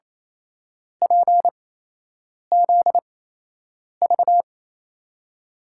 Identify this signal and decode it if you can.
Morse code, heard as a keyed tone, about 27 wpm, spaced as for 8 wpm, 700 Hz. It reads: PZV
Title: USA Call Signs - 5 Characters - 8x Character Spacing - 28wpm